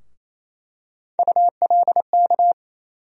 Morse code, heard as a keyed tone, about 27 wpm, 700 Hz. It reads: ULK
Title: USA Call Sign Postfixes - 3 Letters 28wpm